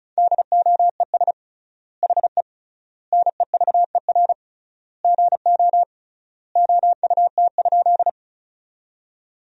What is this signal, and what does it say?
Morse code, heard as a keyed tone, about 34 wpm, 700 Hz. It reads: DOES HE NEVER GO OUT?
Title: 3x Word-Spacing - Sentences from Top 300 Words - 35wpm